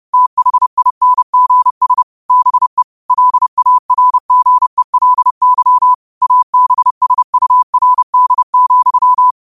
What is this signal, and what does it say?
Morse code, heard as a keyed tone, about 30 wpm, 1000 Hz. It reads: THINGS BE LARGELY ABSURD,
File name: qotdfiles / 30wpm